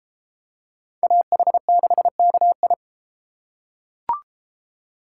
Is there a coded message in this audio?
Yes: Morse code from 1.0 to 2.7 s, heard as a keyed tone, about 31 wpm, 700 Hz. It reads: AH6KI